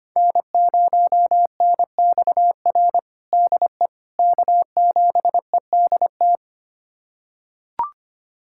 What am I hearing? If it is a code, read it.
Morse code, heard as a keyed tone, about 25 wpm, 700 Hz. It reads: N0NXR DE K7EDT
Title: USA Callsign DE Callsign - 5 Characters - 25wpm